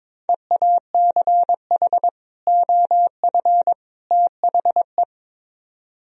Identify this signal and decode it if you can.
Morse code, heard as a keyed tone, about 24 wpm, 700 Hz. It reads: EACH OF THE
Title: Speed-Racing - Sets of 3 Words 22wpm